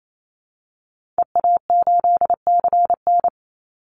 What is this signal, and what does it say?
Morse code, heard as a keyed tone, about 28 wpm, 700 Hz. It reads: EA8CN
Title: Call Signs Repeated 2x - 28wpm